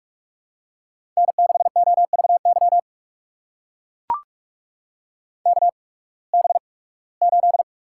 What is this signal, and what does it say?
Morse code, heard as a keyed tone, about 44 wpm, spaced as for 23 wpm, 700 Hz. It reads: N6OVY KB8
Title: USA Call Signs - 8x Character Spacing - 45wpm